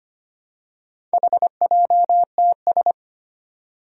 Morse code, heard as a keyed tone, about 25 wpm, 700 Hz. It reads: HJTS
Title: Four Challenging Letter-Number Combinations 25wpm